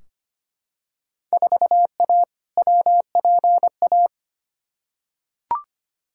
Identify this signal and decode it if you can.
Morse code, heard as a keyed tone, about 25 wpm, 700 Hz. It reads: 4A WPA
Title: Field Day Exchanges Encore - 25wpm